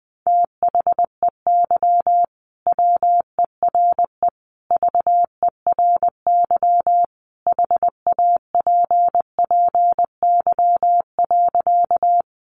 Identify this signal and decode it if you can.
Morse code, heard as a keyed tone, about 20 wpm, 700 Hz. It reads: THEY WERE VERY HAPPY.
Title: Sentences from Top 1000 Words 20wpm